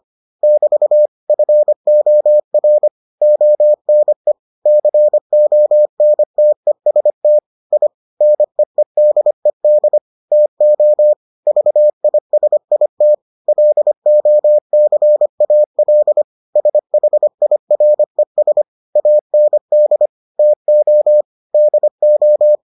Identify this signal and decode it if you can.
Morse code, heard as a keyed tone, about 25 wpm, 600 Hz. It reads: <BT> FOR ONE CONTEST I NEEDED TO VISIT LOCAL SHIRES AND TO DO